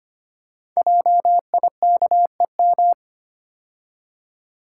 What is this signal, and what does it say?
Morse code, heard as a keyed tone, about 25 wpm, 700 Hz. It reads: JIKEM
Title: Code Groups - 5 Letters - 25wpm